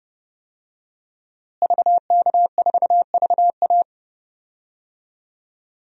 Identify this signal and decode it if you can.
Morse code, heard as a keyed tone, about 32 wpm, 700 Hz. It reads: VK4VA
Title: DXCC-100 Call Signs - 5 Characters - Repeated 3x - 30wpm